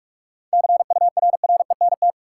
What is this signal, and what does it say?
Morse code, heard as a keyed tone, about 46 wpm, 700 Hz. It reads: CURRENT
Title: Sentences from Top 1000 Words - Review Part 4 - New Word Drill - 45wpm